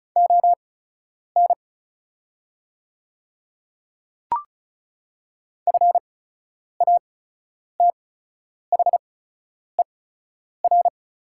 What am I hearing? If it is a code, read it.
Morse code, heard as a keyed tone, about 36 wpm, spaced as for 10 wpm, 700 Hz. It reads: ON FATHER
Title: 8x Character Spacing - Top 300 Words - 35wpm